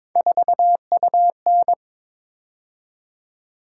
Morse code, heard as a keyed tone, about 22 wpm, 700 Hz. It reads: 4UN
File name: practice / 22wpm / Three Letter-Number Combinations - 22wpm